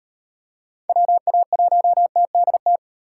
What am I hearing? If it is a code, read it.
Morse code, heard as a keyed tone, about 38 wpm, 700 Hz. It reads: WA1TDT